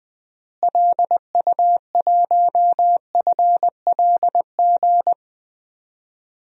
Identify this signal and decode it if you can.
Morse code, heard as a keyed tone, about 20 wpm, 700 Hz. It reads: LU1FLG
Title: DXCC-100 Call Signs - 6 Characters - Repeated 3x - 20wpm